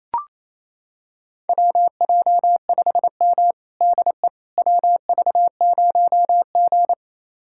Morse code, heard as a keyed tone, about 28 wpm, 700 Hz. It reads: WJ5M DE WV0G
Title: USA Callsign DE Callsign - 4 Characters - 28wpm